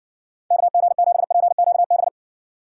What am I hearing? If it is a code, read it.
Morse code, heard as a keyed tone, about 60 wpm, 700 Hz. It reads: KG7PQB